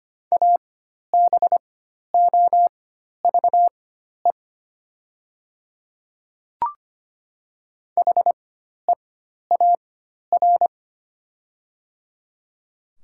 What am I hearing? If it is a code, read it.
Morse code, heard as a keyed tone, about 25 wpm, spaced as for 12 wpm, 700 Hz. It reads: ABOVE HEAR
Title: 4x Character Spacing - Top 500 Words - 25wpm